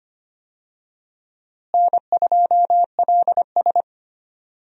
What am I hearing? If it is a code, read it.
Morse code, heard as a keyed tone, about 25 wpm, 700 Hz. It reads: N2LS
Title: USA Call Signs - 4 Characters - Repeated 2x - 25wpm